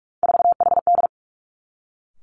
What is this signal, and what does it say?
Morse code, heard as a keyed tone, about 46 wpm, 700 Hz. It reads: 4HD
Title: Three Challenging Letter-Number Combinations 45wpm